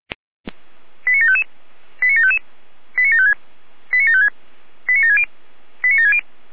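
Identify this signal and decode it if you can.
Five-tone selective call (ZVEI), heard as quick runs of five tones, tones of about 80 ms. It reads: 89755 89755 89756 89756 89760 89760
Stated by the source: ZVEI five-tone selective call received on an SDR.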